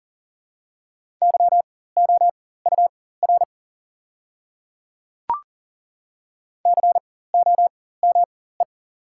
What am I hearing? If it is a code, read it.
Morse code, heard as a keyed tone, about 40 wpm, spaced as for 19 wpm, 700 Hz. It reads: YOUR COME
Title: 4x Character Spacing - Top 200 Words - 40wpm